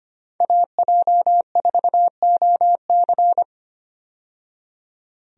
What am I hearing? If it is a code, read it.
Morse code, heard as a keyed tone, about 25 wpm, 700 Hz. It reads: AJ4OC
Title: USA Call Signs - 5 Characters - 25wpm